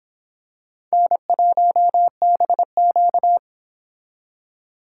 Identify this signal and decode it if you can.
Morse code, heard as a keyed tone, about 26 wpm, 700 Hz. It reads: N1BQ